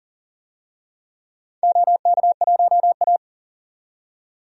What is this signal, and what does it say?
Morse code, heard as a keyed tone, about 40 wpm, 700 Hz. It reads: OK1A
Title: DXCC-100 Call Signs - 4 Characters - 40wpm